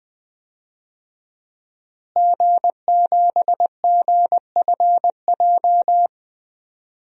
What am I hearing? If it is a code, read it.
Morse code, heard as a keyed tone, about 20 wpm, 700 Hz. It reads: G7GFJ
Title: DXCC-100 Call Signs - 5 Characters - Repeated 3x - 20wpm